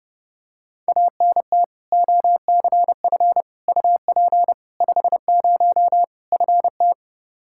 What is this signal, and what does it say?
Morse code, heard as a keyed tone, about 32 wpm, 700 Hz. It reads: ANT OCF UP 50 FT
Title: Practice QSO - Exchange 2 - Info - 30wpm